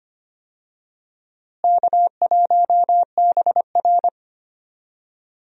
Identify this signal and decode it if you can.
Morse code, heard as a keyed tone, about 25 wpm, 700 Hz. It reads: K1BR
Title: USA Call Signs - 4 Characters - Repeated 3x - 25wpm